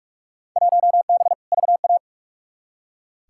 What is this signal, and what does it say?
Morse code, heard as a keyed tone, about 46 wpm, 700 Hz. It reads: 1B VA
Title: Field Day Exchanges - 45wpm